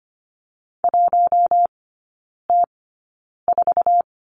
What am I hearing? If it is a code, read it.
Morse code, heard as a keyed tone, about 25 wpm, spaced as for 9 wpm, 690 Hz. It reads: 1T4
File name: Lesson 15e - Number 5 - Three Characters 25wpm